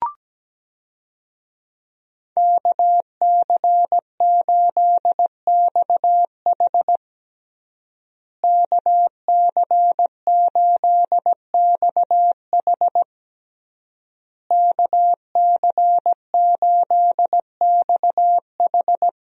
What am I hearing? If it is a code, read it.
Morse code, heard as a keyed tone, about 17 wpm, 700 Hz. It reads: KC8XH KC8XH KC8XH
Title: USA Call Signs - 5 Characters - Repeated 3x - 17wpm